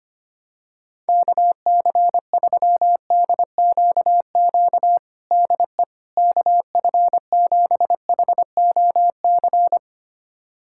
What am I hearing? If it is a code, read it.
Morse code, heard as a keyed tone, about 25 wpm, 700 Hz. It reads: KC3DQQ DE KF7HOC
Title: USA Callsign DE Callsign 25wpm